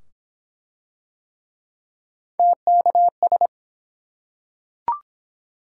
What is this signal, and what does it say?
Morse code, heard as a keyed tone, about 23 wpm, 700 Hz. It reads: TKS